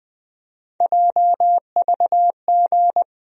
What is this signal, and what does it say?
Morse code, heard as a keyed tone, about 21 wpm, 700 Hz. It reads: JVG
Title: USA Call Sign Postfixes - 3 Letters 20wpm